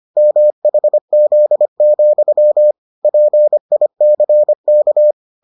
Morse code, heard as a keyed tone, about 25 wpm, 600 Hz. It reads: MHZ, PICK